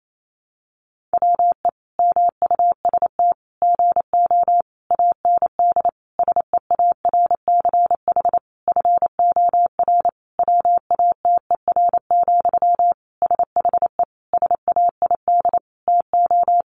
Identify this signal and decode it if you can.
Morse code, heard as a keyed tone, about 28 wpm, 700 Hz. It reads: WE MUST GO AND SEARCH FOR WATER, SHE SAID TO
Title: Sentences from Top 800 Words 28wpm